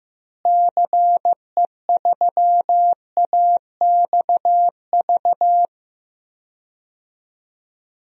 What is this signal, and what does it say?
Morse code, heard as a keyed tone, about 15 wpm, 700 Hz. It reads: CE3AXV
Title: DXCC-100 Call Signs - 6 Characters - Repeated 2x - 15wpm